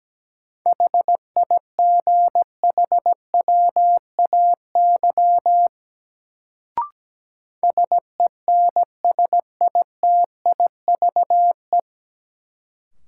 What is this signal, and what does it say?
Morse code, heard as a keyed tone, about 17 wpm, 700 Hz. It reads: HIGHWAY SENSITIVE
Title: Sentences from Top 1400 Words - New Word Drill - 17wpm